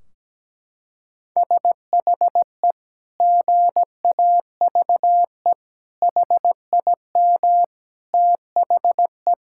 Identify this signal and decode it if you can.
Morse code, heard as a keyed tone, about 17 wpm, 700 Hz. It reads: SHE GAVE HIM THE
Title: Sentences from Top 1600 Words 17wpm